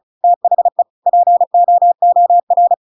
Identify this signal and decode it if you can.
Morse code, heard as a keyed tone, about 34 wpm, 700 Hz. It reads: THE POOR